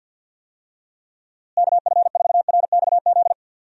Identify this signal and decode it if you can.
Morse code, heard as a keyed tone, about 47 wpm, 700 Hz. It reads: KF4RXZ
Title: USA Call Signs - 6 Characters - Repeated 3x - 50wpm